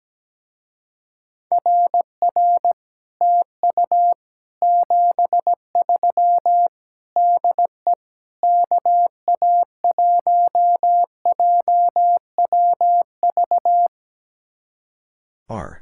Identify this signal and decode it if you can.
Morse code, heard as a keyed tone, about 17 wpm, 700 Hz. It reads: RR TU 73 DE KA1JWV
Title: Speed-Racing Common QSO Phrases - 17wpm